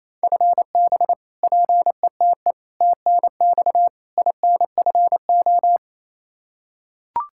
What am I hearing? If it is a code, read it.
Morse code, heard as a keyed tone, about 28 wpm, 700 Hz. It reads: FB PETE TNX INFO
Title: Practice QSO - Exchange 3 - More Info - 28wpm